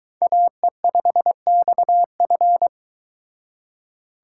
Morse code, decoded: AE5XF

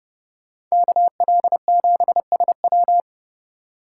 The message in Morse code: KL7SW

